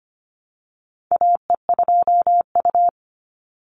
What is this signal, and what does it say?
Morse code, heard as a keyed tone, about 25 wpm, 700 Hz. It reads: AE2U
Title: Call Signs - 25wpm